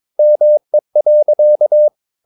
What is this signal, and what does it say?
Morse code, heard as a keyed tone, about 22 wpm, 600 Hz. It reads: ME.
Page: Romanen Never af Ken Follet på engelsk.